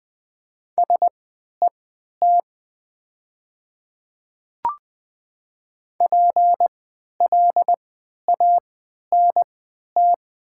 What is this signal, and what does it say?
Morse code, heard as a keyed tone, about 20 wpm, spaced as for 11 wpm, 700 Hz. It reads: SET PLANT
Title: 3x Character Spacing - Top 500 Words - 20wpm